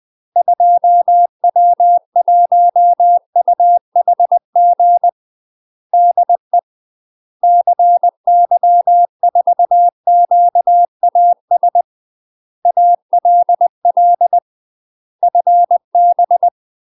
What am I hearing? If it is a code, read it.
Morse code, heard as a keyed tone, about 20 wpm, 700 Hz. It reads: 2W1UHG DE CY4QAS ALL FB
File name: UK MASTER QSOs TRAINING / 20WPM / UK MASTER QSOs 0046